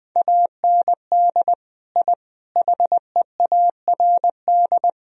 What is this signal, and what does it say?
Morse code, heard as a keyed tone, about 20 wpm, 700 Hz. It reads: AND I HEARD
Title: Sentences from Top 1000 Words - Review Part 1 - 20wpm